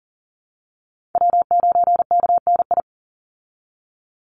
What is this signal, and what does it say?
Morse code, heard as a keyed tone, about 40 wpm, 700 Hz. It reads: W9KNI
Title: Call Signs Repeated 3x - 40wpm